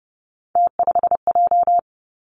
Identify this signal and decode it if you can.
Morse code, heard as a keyed tone, about 31 wpm, 700 Hz. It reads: T5J
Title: Three Challenging Letter-Number Combinations 30wpm